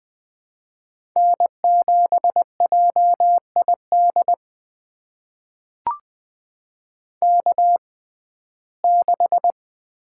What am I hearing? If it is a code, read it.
Morse code, heard as a keyed tone, about 20 wpm, spaced as for 14 wpm, 700 Hz. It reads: N7JID K6
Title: USA Call Signs - 5 Characters - 6x Character Spacing - 20wpm